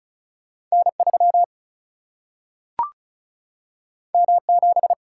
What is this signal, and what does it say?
Morse code, heard as a keyed tone, about 36 wpm, 700 Hz. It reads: N 3 M 7
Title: Single-Letter Single-Number 35wpm